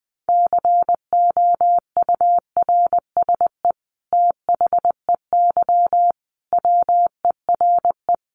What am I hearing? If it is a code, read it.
Morse code, heard as a keyed tone, about 20 wpm, 700 Hz. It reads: COURSE THEY WERE
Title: Sentences from Top 900 Words 20wpm